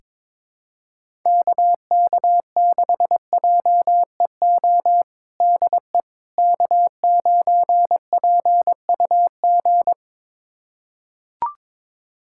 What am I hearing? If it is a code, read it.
Morse code, heard as a keyed tone, about 22 wpm, 700 Hz. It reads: KK6JEO DE K9PUG
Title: USA Callsign DE Callsign 22wpm